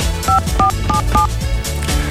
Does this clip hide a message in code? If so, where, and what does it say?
Yes: DTMF from 0.3 to 1.3 s, heard as telephone touch tones, tones of about 120 ms. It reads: 5477